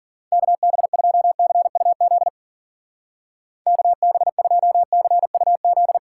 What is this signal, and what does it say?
Morse code, heard as a keyed tone, about 42 wpm, 700 Hz. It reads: KB2CUZ KB2CUZ
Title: Speed Racing - USA 2x3 Call Signs 40wpm